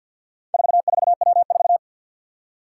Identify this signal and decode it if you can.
Morse code, heard as a keyed tone, about 52 wpm, 700 Hz. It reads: 44W4
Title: Four Challenging Letter-Number Combinations 50wpm